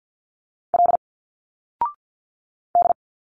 Morse code, heard as a keyed tone, about 48 wpm, 700 Hz. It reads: L D
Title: Single Letters 50wpm